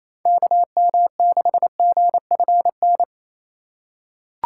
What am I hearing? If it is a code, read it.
Morse code, heard as a keyed tone, about 28 wpm, 700 Hz. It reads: KM6GFN
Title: USA Call Signs - 6 Characters - 3x Character Spacing - 28wpm